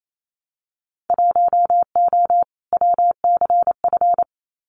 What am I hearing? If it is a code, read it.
Morse code, heard as a keyed tone, about 28 wpm, 700 Hz. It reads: 1O WCF
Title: Winter Field Day Exchanges 28wpm